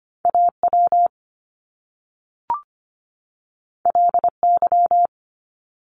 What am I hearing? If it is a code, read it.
Morse code, heard as a keyed tone, about 25 wpm, 700 Hz. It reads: AW LY
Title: Top Two Letter Combinations in Words 25wpm